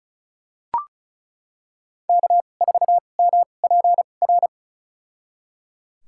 Morse code, heard as a keyed tone, about 34 wpm, spaced as for 25 wpm, 700 Hz. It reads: K4MPR
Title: USA Call Signs - 5 Characters - 2x Character Spacing - 35wpm